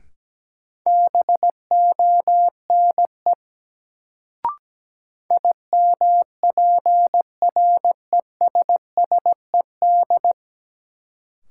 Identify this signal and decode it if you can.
Morse code, heard as a keyed tone, about 17 wpm, 700 Hz. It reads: BONE IMPRESSED
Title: Sentences from Top 1600 Words - New Word Drill Alt - 17wpm